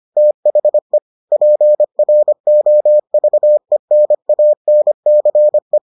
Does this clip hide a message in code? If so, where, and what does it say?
Yes: Morse code from 0.2 to 5.8 s, heard as a keyed tone, about 25 wpm, 600 Hz. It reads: THE PROVENANCE